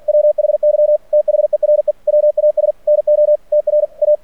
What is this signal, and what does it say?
Morse code, heard as a keyed tone, about 49 wpm, 600 Hz. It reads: WHY THERE WAS NO TRA
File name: archived / CRSnet